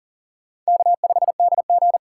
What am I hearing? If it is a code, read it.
Morse code, heard as a keyed tone, about 40 wpm, 700 Hz. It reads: K5DG
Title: USA Call Signs - 4 Characters - 4x Character Spacing - 40wpm